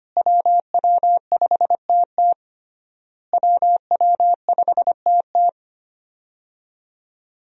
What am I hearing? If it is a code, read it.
Morse code, heard as a keyed tone, about 25 wpm, 700 Hz. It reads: WW5TT WW5TT
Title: USA Call Signs - 5 Characters - Repeated 3x - 25wpm